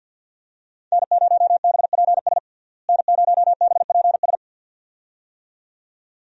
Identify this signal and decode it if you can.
Morse code, heard as a keyed tone, about 52 wpm, 700 Hz. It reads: N0BPS N0BPS
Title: USA Call Signs - 5 Characters - Repeated 2x - 50wpm